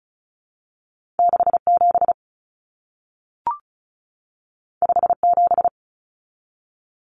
Morse code, heard as a keyed tone, about 36 wpm, 700 Hz. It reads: 67 57
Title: Common Cut Numbers - 2-Digit 35wpm